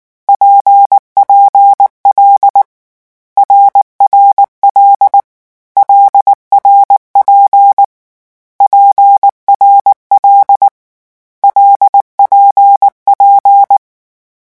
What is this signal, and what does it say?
Morse code, heard as a keyed tone, about 19 wpm, 800 Hz. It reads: PPL RRL LRP PRL LPP
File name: morse mp3 / lecon6-vitesse 18